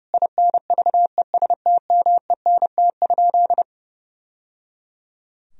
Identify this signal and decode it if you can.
Morse code, heard as a keyed tone, about 30 wpm, 700 Hz. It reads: INVESTMENT?
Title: Sentences from Top 1000 Words - Review Part 4 - 30wpm